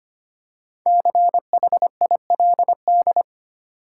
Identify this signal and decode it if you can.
Morse code, heard as a keyed tone, about 25 wpm, 700 Hz. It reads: CHILD
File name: 5 Letter Words - First 3 Letters Given - 25wpm